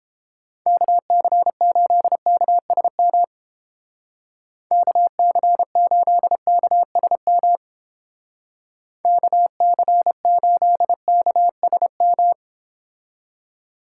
Morse code, heard as a keyed tone, about 29 wpm, 700 Hz. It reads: KC8KSM KC8KSM KC8KSM